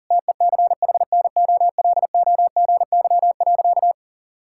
Morse code, heard as a keyed tone, about 40 wpm, 700 Hz. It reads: TECHNOLOGY.
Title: Sentences from Top 1000 Words - Review Part 5 - 40wpm